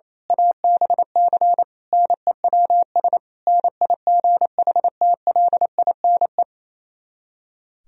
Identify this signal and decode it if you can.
Morse code, heard as a keyed tone, about 27 wpm, 700 Hz. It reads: ABC NEWS NIGHTLINE
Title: Speed-Racing - Sets of 5 Words 28wpm